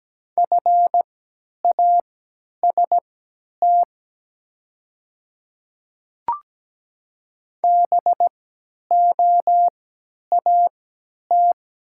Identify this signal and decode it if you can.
Morse code, heard as a keyed tone, about 17 wpm, spaced as for 10 wpm, 700 Hz. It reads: FAST BOAT
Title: 3x Character Spacing - Top 500 Words - 17wpm